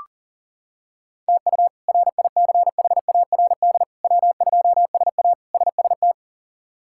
Morse code, heard as a keyed tone, about 37 wpm, 700 Hz. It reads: TU RICHARD W2SA SST
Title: K1USN SST QSO Practice - 40wpm